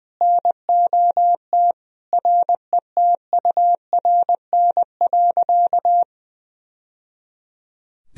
Morse code, heard as a keyed tone, about 20 wpm, 700 Hz. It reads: NOT RETURN.
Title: Sentences from Top 1000 Words - Review Part 1 - 20wpm